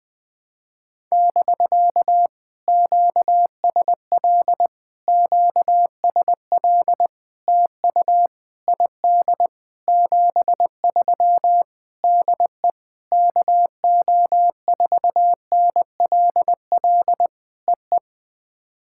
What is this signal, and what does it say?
Morse code, heard as a keyed tone, about 20 wpm, 700 Hz. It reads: <BK> QSL QSL TU ID 73 DE KO4NLL EE
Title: POTA QSO Practice - 20wpm